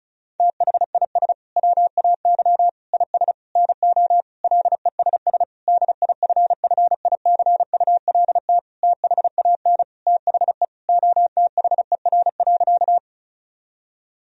Morse code, heard as a keyed tone, about 34 wpm, 700 Hz. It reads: THIS WAY IS NO LESS DIFFICULT THAN THE OTHER.
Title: Sentences from Top 1000 Words - Review Part 2 - 35wpm